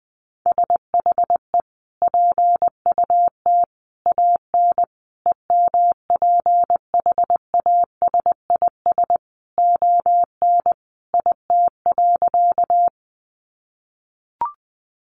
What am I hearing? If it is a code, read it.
Morse code, heard as a keyed tone, about 20 wpm, 700 Hz. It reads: SHE PUT AN EMPHASIS ON IT.
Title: Sentences from Top 1400 Words 20wpm